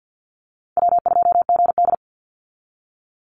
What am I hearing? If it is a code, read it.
Morse code, heard as a keyed tone, about 50 wpm, 700 Hz. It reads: W2GD
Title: Easy Call Signs - 50wpm